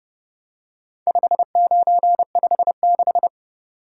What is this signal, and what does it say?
Morse code, heard as a keyed tone, about 30 wpm, 700 Hz. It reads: 5956